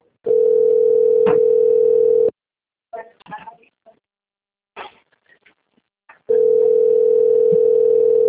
Ringback tone (North America), 440 + 480 Hz.
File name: rg-1600-unknown- - - .2776